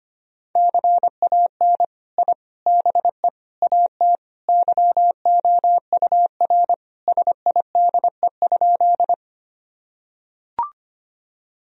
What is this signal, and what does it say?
Morse code, heard as a keyed tone, about 25 wpm, 700 Hz. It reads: CAN I BE AT YOUR SIDE?
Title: Sentences from Top 1000 Words - Review Part 1 - 25wpm